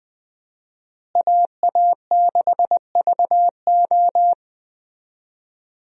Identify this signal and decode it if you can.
Morse code, heard as a keyed tone, about 20 wpm, 700 Hz. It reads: AA6VO
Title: USA Call Signs - 5 Characters - Repeated 2x - 20wpm